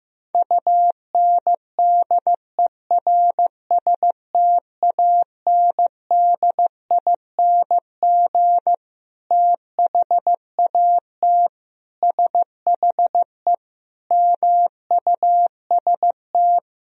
Morse code, heard as a keyed tone, about 15 wpm, 700 Hz. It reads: UNDERSTANDING THAT SHE MUST
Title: Sentences from Top 1000 Words - Review Part 3 - 15wpm